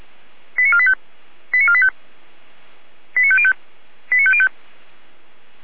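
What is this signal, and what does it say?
Five-tone selective call (ZVEI), heard as quick runs of five tones, tones of about 70 ms. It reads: 89485 89485 89505 89505
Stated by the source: ZVEI five-tone selective call 89017, received by SDR.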